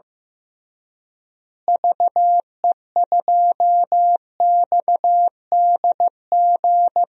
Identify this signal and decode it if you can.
Morse code, heard as a keyed tone, about 16 wpm, 700 Hz. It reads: VE2XDG
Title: DXCC-100 Call Signs - 6 Characters - Repeated 3x - 15wpm